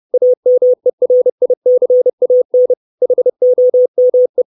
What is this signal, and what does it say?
Morse code, heard as a keyed tone, about 30 wpm, 500 Hz. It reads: AMERICAN HOME
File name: 632 Words 30WPM